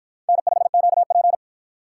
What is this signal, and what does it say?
Morse code, heard as a keyed tone, about 53 wpm, 700 Hz. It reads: N5QP